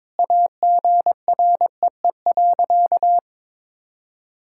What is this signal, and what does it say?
Morse code, heard as a keyed tone, about 21 wpm, 700 Hz. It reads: AGREE.
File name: practice / 22wpm / Sentences with 4-Word Phrases - 22wpm